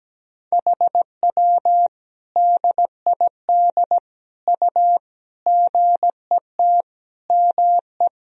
Morse code, heard as a keyed tone, about 17 wpm, 700 Hz. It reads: HW DID U GET ME
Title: Sentences from Top 200 Words and Abbreviations 17wpm